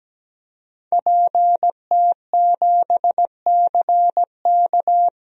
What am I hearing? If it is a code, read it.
Morse code, heard as a keyed tone, about 17 wpm, 700 Hz. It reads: PT7CK
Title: DXCC-100 Call Signs - 5 Characters - Repeated 2x - 17wpm